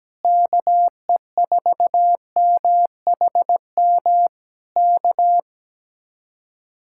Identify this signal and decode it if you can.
Morse code, heard as a keyed tone, about 17 wpm, 700 Hz. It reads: KE4MHM K
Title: Speed-Racing Common QSO Phrases - 17wpm